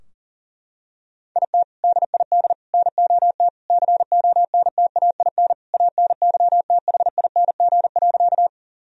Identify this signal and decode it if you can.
Morse code, heard as a keyed tone, about 40 wpm, 700 Hz. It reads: IT DID NOT CONTAIN ANYTHING.
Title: Sentences from Top 1000 Words - Review Part 2 - 40wpm